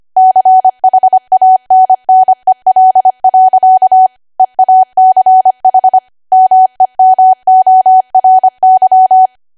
Morse code, heard as a keyed tone, about 25 wpm, 750 Hz. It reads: CHANNEL. EACH MEMORY
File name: Archive / 25 WPM / 25WPM